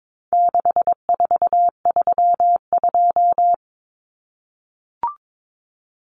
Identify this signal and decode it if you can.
Morse code, heard as a keyed tone, about 22 wpm, 700 Hz. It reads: 6432
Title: Common Cut Numbers - 4-Digit 22wpm